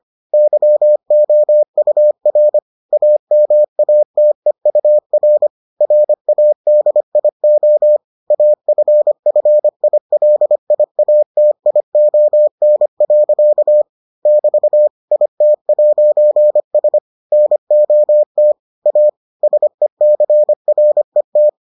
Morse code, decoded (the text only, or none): YOUR AMATEUR RADIO AFFILIATION. <BT> IT'S NOT A SECRET